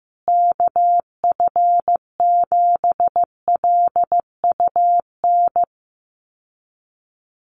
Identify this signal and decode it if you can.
Morse code, heard as a keyed tone, about 15 wpm, 700 Hz. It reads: KF7LUN
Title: USA Call Signs Repeated 3x 15wpm